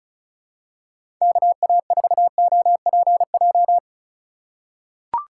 Morse code, heard as a keyed tone, about 36 wpm, 700 Hz. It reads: KA4OPJ
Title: USA Call Signs - 6 Characters - 2x Character Spacing - 35wpm